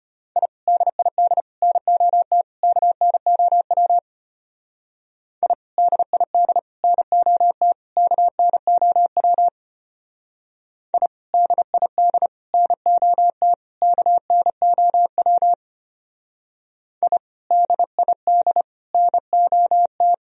Morse code, heard as a keyed tone, about 31 wpm, 700 Hz. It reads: I DID NOT KNOW I DID NOT KNOW I DID NOT KNOW I DID NOT
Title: Speed-Racing - Sets of 4 Words 25wpm